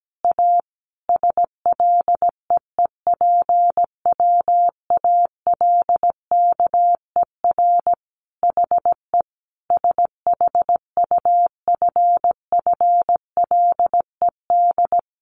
Morse code, heard as a keyed tone, about 17 wpm, 700 Hz. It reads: A SLEEPWALKER HE SHUFFLED
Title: A Sound Of Thunder 17wpm